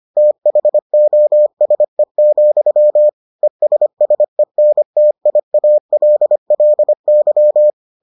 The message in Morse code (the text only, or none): THOSE, ESSENTIALLY